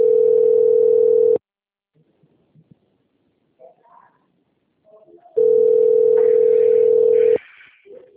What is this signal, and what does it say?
Ringback tone (North America), 440 + 480 Hz.
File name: out- - - - - .9823